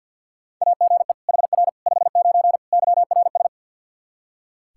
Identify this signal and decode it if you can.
Morse code, heard as a keyed tone, about 45 wpm, 700 Hz. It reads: AGE HR 59 YRS
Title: Practice QSO - Exchange 3 - More Info - 50wpm